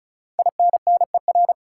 Morse code, heard as a keyed tone, about 36 wpm, 700 Hz. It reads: INNER
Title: Five Letter Words - 35wpm